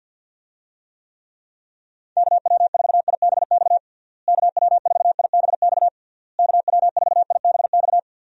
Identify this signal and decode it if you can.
Morse code, heard as a keyed tone, about 52 wpm, 700 Hz. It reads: KW4IBX KW4IBX KW4IBX
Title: USA Call Signs - 6 Characters - Repeated 3x - 50wpm